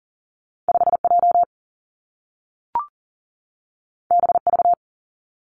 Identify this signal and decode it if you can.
Morse code, heard as a keyed tone, about 40 wpm, 700 Hz. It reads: 5 J B V
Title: Two Challenging Letter-Number Combinations 40wpm